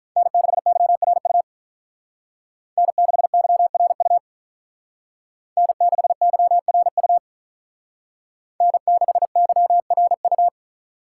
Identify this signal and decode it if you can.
Morse code, heard as a keyed tone, about 45 wpm, 700 Hz. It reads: N6YRU N6YRU N6YRU N6YRU